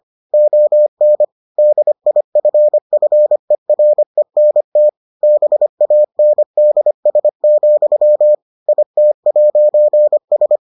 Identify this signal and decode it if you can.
Morse code, heard as a keyed tone, about 25 wpm, 600 Hz. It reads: ON DIFFERENT BANDS, IT'S